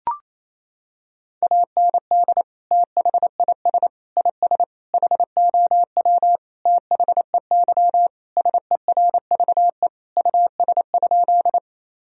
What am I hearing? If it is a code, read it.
Morse code, heard as a keyed tone, about 28 wpm, 700 Hz. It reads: AND THIS IS HOW THEY SERVE US?
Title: Sentences from Top 1000 Words - Review Part 2 - 28wpm